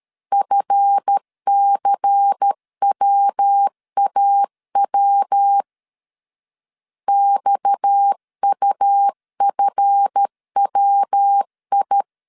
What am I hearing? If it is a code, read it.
Morse code, heard as a keyed tone, about 13 wpm, 800 Hz. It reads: FCWAW XUFWI
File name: LEZIONI / 7 GF9Z / 7 GF 12wpm 2122